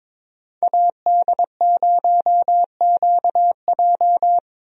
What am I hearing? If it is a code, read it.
Morse code, heard as a keyed tone, about 22 wpm, 700 Hz. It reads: AD0QJ
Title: USA Call Signs - 5 Characters - Repeated 2x - 22wpm